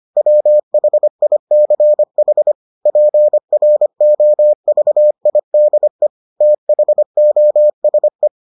Morse code, heard as a keyed tone, about 25 wpm, 600 Hz. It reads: WHICH PROVIDE THOSE